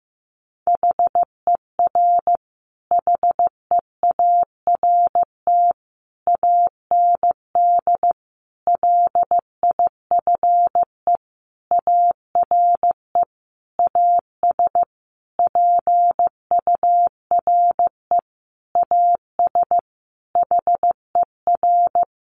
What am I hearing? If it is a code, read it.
Morse code, heard as a keyed tone, about 15 wpm, 700 Hz. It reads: HER HEART AND LIFE ARE AS PURE AS HER
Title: Sentences from Top 1500 Words 15wpm